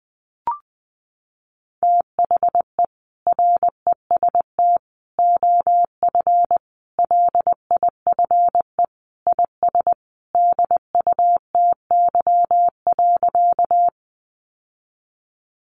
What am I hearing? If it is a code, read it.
Morse code, heard as a keyed tone, about 20 wpm, 700 Hz. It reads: THE REST OF LIFE IS DUTY.
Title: Sentences from Top 1500 Words 20wpm